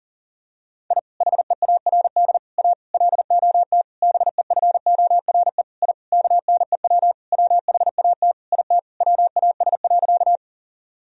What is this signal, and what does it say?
Morse code, heard as a keyed tone, about 40 wpm, 700 Hz. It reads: I HEARD A LOT BEFORE I KNEW WHAT IT WAS.
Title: Sentences from Top 1000 Words - Review Part 3 - 40wpm